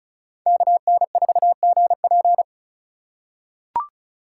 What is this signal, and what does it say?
Morse code, heard as a keyed tone, about 36 wpm, 700 Hz. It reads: KN4GP
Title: USA Call Signs - 5 Characters - 2x Character Spacing - 35wpm